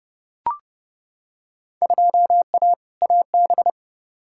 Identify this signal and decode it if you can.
Morse code, heard as a keyed tone, about 31 wpm, 700 Hz. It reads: 2A AB
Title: Field Day Exchanges - 30wpm